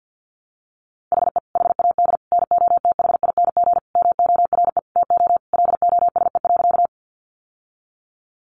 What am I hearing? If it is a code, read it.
Morse code, heard as a keyed tone, about 50 wpm, 700 Hz. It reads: HE HAD NOTHING MORE TO LOSE.
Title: Sentences from Top 1100 Words 50wpm